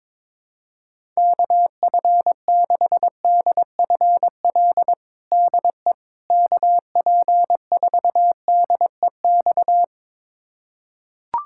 Morse code, heard as a keyed tone, about 22 wpm, 700 Hz. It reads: KF6DFL DE KP4DEX
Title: USA Callsign DE Callsign - 6 Characters - 22wpm